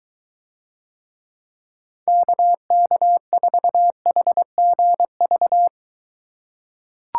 Morse code, heard as a keyed tone, about 23 wpm, 700 Hz. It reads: KK4HGV